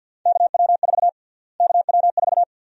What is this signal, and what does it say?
Morse code, heard as a keyed tone, about 52 wpm, 700 Hz. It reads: KW4 KW4
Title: USA Call Sign Prefixes - Repeated 2x - 50wpm